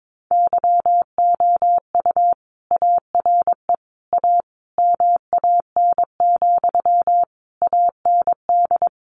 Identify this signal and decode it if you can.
Morse code, heard as a keyed tone, about 22 wpm, 700 Hz. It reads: YOU ARE A MAN, AND
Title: Sentences from Top 200 Words 22wpm